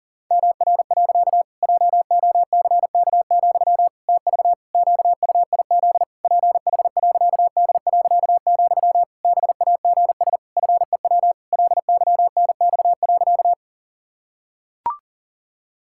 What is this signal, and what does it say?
Morse code, heard as a keyed tone, about 37 wpm, 700 Hz. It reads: MR. JOCK, TV QUIZ PH.D., BAGS FEW LYNX.
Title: Pangram Challenge v2 - 40wpm